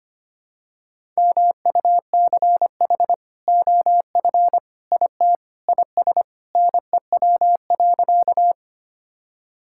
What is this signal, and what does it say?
Morse code, heard as a keyed tone, about 25 wpm, 700 Hz. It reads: MUCH OF IT IS NEW.
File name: practice / 25wpm / Sentences with 2-Word Phrases - 25wpm